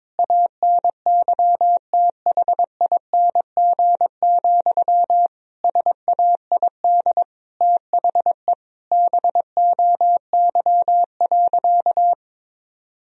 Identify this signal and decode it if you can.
Morse code, heard as a keyed tone, about 22 wpm, 700 Hz. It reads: ANYTHING, SAID THE BOY.
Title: Sentences from Top 1000 Words - Review Part 4 - 22wpm